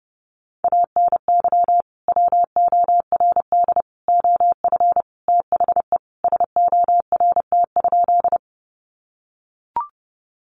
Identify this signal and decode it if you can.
Morse code, heard as a keyed tone, about 30 wpm, 700 Hz. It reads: ANY WORD OF THE SORT?
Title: Sentences from Top 1300 Words 30wpm